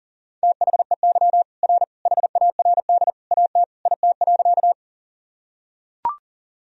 Morse code, heard as a keyed tone, about 40 wpm, 700 Hz. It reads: THEY R HARD AT IT.
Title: Sentences from Top 300 Words and Abbreviations 40wpm